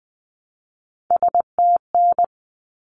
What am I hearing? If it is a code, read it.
Morse code, heard as a keyed tone, about 20 wpm, 700 Hz. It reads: STN
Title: Abbreviations 20wpm